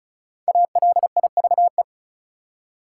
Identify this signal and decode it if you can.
Morse code, heard as a keyed tone, about 36 wpm, 700 Hz. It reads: ALIVE
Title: Five Letter Words - 35wpm